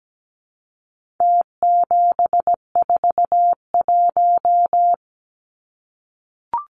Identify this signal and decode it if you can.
Morse code, heard as a keyed tone, about 17 wpm, 700 Hz. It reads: T741
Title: Numbers - Contest Style - 4-Digit 17wpm